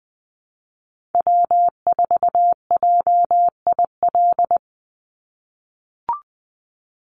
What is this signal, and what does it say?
Morse code, heard as a keyed tone, about 20 wpm, 700 Hz. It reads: W4JIL
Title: USA Call Signs 20wpm